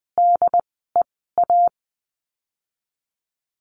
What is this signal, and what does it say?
Morse code, heard as a keyed tone, about 20 wpm, 700 Hz. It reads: D E A
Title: 068 Lesson 12f - Letter d - Words 20wpm